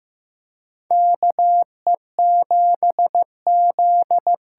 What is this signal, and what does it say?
Morse code, heard as a keyed tone, about 15 wpm, 700 Hz. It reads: KE7Z
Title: USA Call Signs - 6 Characters - 8x Character Spacing - 15wpm